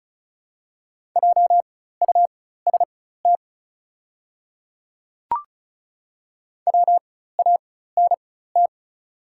Morse code, heard as a keyed tone, about 34 wpm, spaced as for 16 wpm, 700 Hz. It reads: JUST WANT
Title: 4x Character Spacing - Top 200 Words - 35wpm